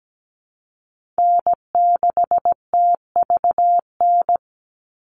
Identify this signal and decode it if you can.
Morse code, heard as a keyed tone, about 17 wpm, 700 Hz. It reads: N6TVN
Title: Call Signs - 17wpm